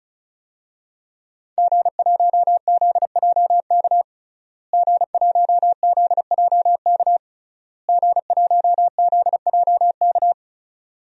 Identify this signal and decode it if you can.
Morse code, heard as a keyed tone, about 34 wpm, 700 Hz. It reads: G1ZJK G1ZJK G1ZJK
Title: DXCC-100 Call Signs - 5 Characters - Repeated 3x - 35wpm